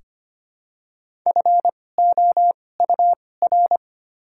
Morse code, heard as a keyed tone, about 25 wpm, spaced as for 18 wpm, 700 Hz. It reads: FOUR
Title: 2x Character Spacing - Top 500 Words - 25wpm